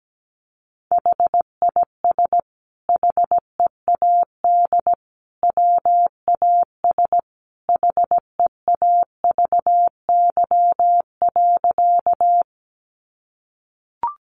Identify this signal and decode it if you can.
Morse code, heard as a keyed tone, about 17 wpm, 700 Hz. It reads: HIS HEAD WAS HEAVY.
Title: Sentences from Top 1100 Words 17wpm